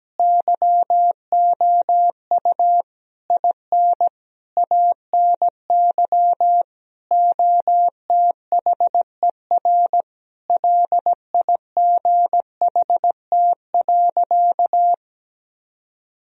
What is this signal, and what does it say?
Morse code, heard as a keyed tone, about 17 wpm, 700 Hz. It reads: YOU IN ANY OTHER LIGHT.
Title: Sentences from Top 1000 Words - Review Part 1 - 17wpm